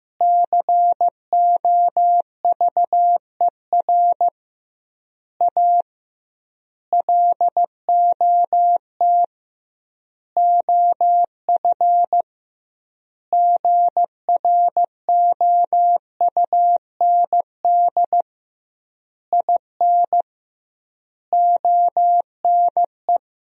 Morse code, heard as a keyed tone, about 15 wpm, 700 Hz. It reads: COVER A LOT OF GROUND IN ONE
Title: 2x Word-Spacing - Sentences from Top 500 Words - 15wpm